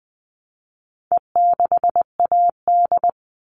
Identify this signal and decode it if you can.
Morse code, heard as a keyed tone, about 20 wpm, 700 Hz. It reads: E6AD
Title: All Cut Numbers - 4-Digit 20wpm